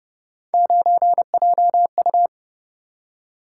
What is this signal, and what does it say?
Morse code, heard as a keyed tone, about 30 wpm, 700 Hz. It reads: 9JU